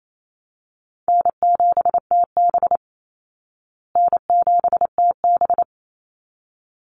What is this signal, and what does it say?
Morse code, heard as a keyed tone, about 28 wpm, 700 Hz. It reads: N7TB N7TB
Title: Call Signs Repeated 3x - 28wpm